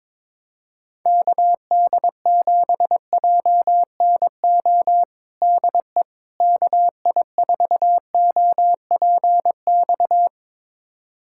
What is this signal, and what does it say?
Morse code, heard as a keyed tone, about 22 wpm, 700 Hz. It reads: KD7JNO DE KI4OPX
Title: USA Callsign DE Callsign 22wpm